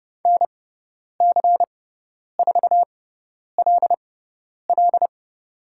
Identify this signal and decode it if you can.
Morse code, heard as a keyed tone, about 30 wpm, spaced as for 10 wpm, 700 Hz. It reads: NC4LL
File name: Lesson 16g - Letter c - Call Signs 30wpm